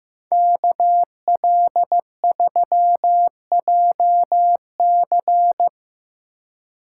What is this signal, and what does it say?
Morse code, heard as a keyed tone, about 16 wpm, 700 Hz. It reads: KL3JC
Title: Speed Racing - USA 2x2 Call Signs 15wpm